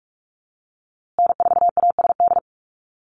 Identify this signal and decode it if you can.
Morse code, heard as a keyed tone, about 44 wpm, 700 Hz. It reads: N4ASD
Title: USA Call Signs Repeated 2x 45wpm